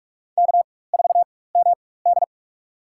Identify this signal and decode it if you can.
Morse code, heard as a keyed tone, about 44 wpm, spaced as for 21 wpm, 700 Hz. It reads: K4MD
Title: USA Call Signs - 4 Characters - 4x Character Spacing - 45wpm